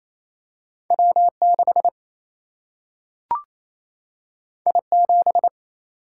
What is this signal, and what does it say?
Morse code, heard as a keyed tone, about 28 wpm, 700 Hz. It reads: W6 I7
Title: Single-Letter Single-Number 28wpm